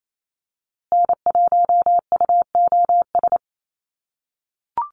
Morse code, heard as a keyed tone, about 28 wpm, 700 Hz. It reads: N1UOS
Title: USA Call Signs Repeated 3x 28wpm